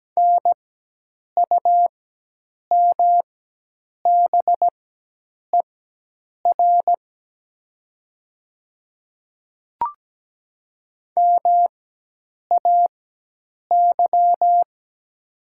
Morse code, heard as a keyed tone, about 17 wpm, spaced as for 8 wpm, 700 Hz. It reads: NUMBER MAY
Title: 4x Character Spacing - Top 100 Words - 17wpm